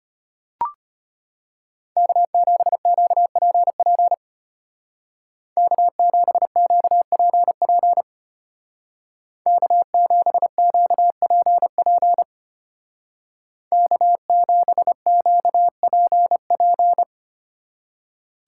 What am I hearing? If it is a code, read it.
Morse code, heard as a keyed tone, about 33 wpm, 700 Hz. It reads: K7QPP K7QPP K7QPP K7QPP